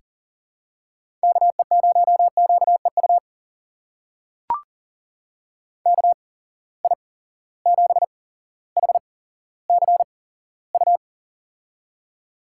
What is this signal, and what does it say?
Morse code, heard as a keyed tone, about 40 wpm, spaced as for 17 wpm, 700 Hz. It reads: KE0QEU KI7HCU
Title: USA Call Signs - 6 Characters - 8x Character Spacing - 40wpm